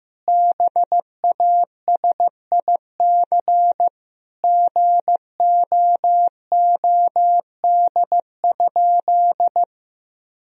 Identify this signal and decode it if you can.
Morse code, heard as a keyed tone, about 14 wpm, 700 Hz. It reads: BASIC GOOD?
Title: Sentences from Top 1000 Words - Review Part 3 - 15wpm